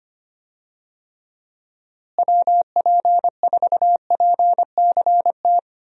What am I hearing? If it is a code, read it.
Morse code, heard as a keyed tone, about 25 wpm, 700 Hz. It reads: WP4PCT
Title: USA Call Signs - 6 Characters - Repeated 3x - 25wpm